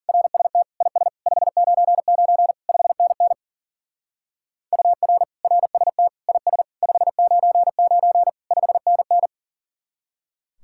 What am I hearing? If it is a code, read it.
Morse code, heard as a keyed tone, about 49 wpm, 700 Hz. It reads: RST IS 599 5NN UR RST IS 599 5NN
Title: Speed-Racing Common QSO Phrases - 40wpm